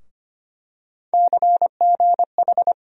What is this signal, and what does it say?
Morse code, heard as a keyed tone, about 25 wpm, 700 Hz. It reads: CGH
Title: USA Call Sign Postfixes - 3 Letters 25wpm